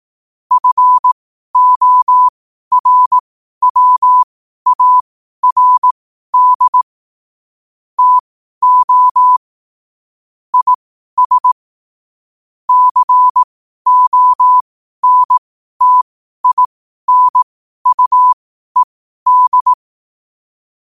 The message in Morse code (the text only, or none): FORWARD TO IS CONTINUED